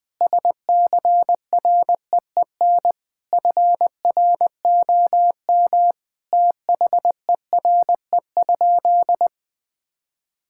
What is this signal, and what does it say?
Morse code, heard as a keyed tone, about 21 wpm, 700 Hz. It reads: SCREEN FROM THERE?
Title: Sentences from Top 1000 Words - Review Part 6 - 20wpm